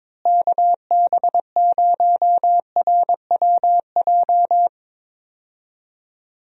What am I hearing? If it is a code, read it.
Morse code, heard as a keyed tone, about 22 wpm, 700 Hz. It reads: KB0RWJ
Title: USA Call Signs - 6 Characters - Repeated 2x - 22wpm